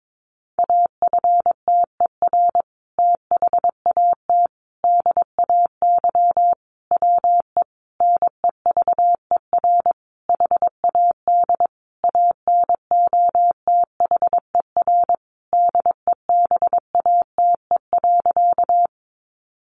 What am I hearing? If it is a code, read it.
Morse code, heard as a keyed tone, about 22 wpm, 700 Hz. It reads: AFTER THAT DAY WE NEVER HAD ANOTHER DEBATE.
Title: Sentences from Top 1100 Words 22wpm